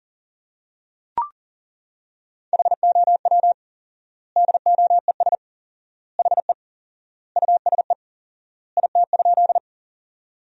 Morse code, heard as a keyed tone, about 37 wpm, 700 Hz. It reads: HOW DOES HE USE IT?
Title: 4x Word-Spacing - Sentences from Top 200 Words - 40wpm